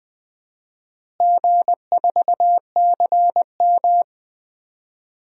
Morse code, heard as a keyed tone, about 20 wpm, 700 Hz. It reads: G4CM
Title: DXCC-100 Call Signs - 4 Characters - Repeated 3x - 20wpm